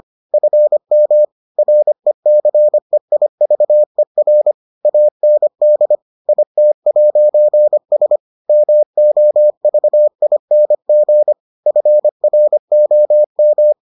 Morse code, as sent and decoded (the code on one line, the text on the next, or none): ..-. -- / .-. -.-. .. ...- . .-. / .- -. -.. / .. - .----. ... / -- --- ...- .. -. --. / ..-. .-. --- --
FM RECEIVER AND IT'S MOVING FROM